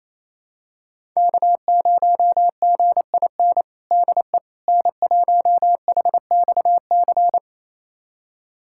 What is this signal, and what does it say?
Morse code, heard as a keyed tone, about 28 wpm, 700 Hz. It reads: K0GIN DE N1HXC
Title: USA Callsign DE Callsign - 5 Characters - 28wpm